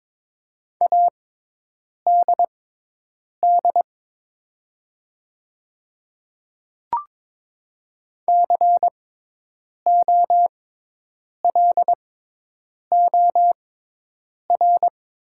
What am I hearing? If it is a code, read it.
Morse code, heard as a keyed tone, about 22 wpm, spaced as for 8 wpm, 690 Hz. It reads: ADD COLOR
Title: 6x Character Spacing - Top 500 Words - 22wpm